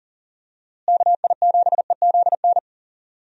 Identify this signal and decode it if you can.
Morse code, heard as a keyed tone, about 40 wpm, 700 Hz. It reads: KI7EZN
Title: USA Call Signs - 6 Characters - Repeated 2x - 40wpm